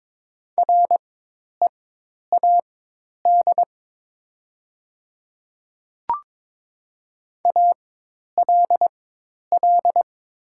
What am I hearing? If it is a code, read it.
Morse code, heard as a keyed tone, about 22 wpm, spaced as for 10 wpm, 700 Hz. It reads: READ ALL
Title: 4x Character Spacing - Top 300 Words - 22wpm